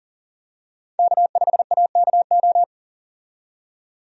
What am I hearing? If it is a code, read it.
Morse code, heard as a keyed tone, about 38 wpm, 700 Hz. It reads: K5AKO
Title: USA Call Signs - 5 Characters - 40wpm